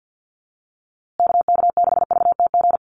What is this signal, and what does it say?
Morse code, heard as a keyed tone, about 49 wpm, 700 Hz. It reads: KK6VTG
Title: USA Call Signs Repeated 2x 50wpm